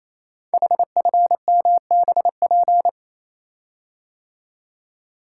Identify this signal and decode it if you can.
Morse code, heard as a keyed tone, about 27 wpm, 700 Hz. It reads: HFMBP
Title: Code Groups - 5 Letter:Numbers - 28wpm